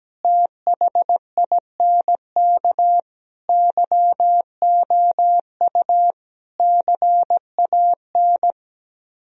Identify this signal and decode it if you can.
Morse code, heard as a keyed tone, about 18 wpm, 700 Hz. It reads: THINK YOU CAN
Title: Speed-Racing - Sets of 5 Words 17wpm